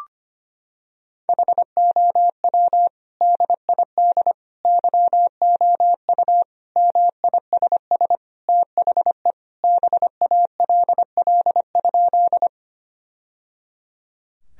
Morse code, heard as a keyed tone, about 25 wpm, 700 Hz. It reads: HOW DID YOU MISS THE BALL?
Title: Sentences from Top 1000 Words - Review Part 3 - 25wpm